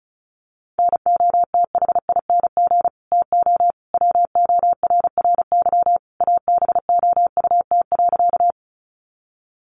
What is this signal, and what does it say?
Morse code, heard as a keyed tone, about 34 wpm, 700 Hz. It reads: NOTHING TO WORRY ABOUT.
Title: Sentences from Top 1100 Words 35wpm